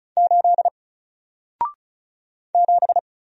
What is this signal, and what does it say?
Morse code, heard as a keyed tone, about 36 wpm, 700 Hz. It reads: 8 7
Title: Single Character (Letter:Number:Punctuation:Prosign) Alt 35wpm